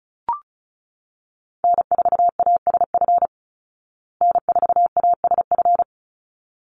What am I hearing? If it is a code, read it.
Morse code, heard as a keyed tone, about 34 wpm, 700 Hz. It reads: N4ASF N4ASF
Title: Call Signs Repeated 3x - 35wpm